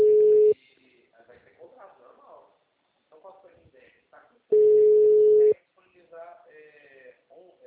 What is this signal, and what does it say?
Ringback tone (Europe, 425 Hz), with speech.